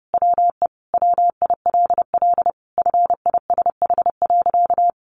Morse code, heard as a keyed tone, about 30 wpm, 700 Hz. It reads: WE WILL FISH.
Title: Sentences from Top 1100 Words 30wpm